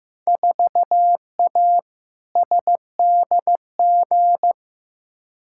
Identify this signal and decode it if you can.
Morse code, heard as a keyed tone, about 15 wpm, 700 Hz. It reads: 4A SDG
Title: Field Day Exchanges - 15wpm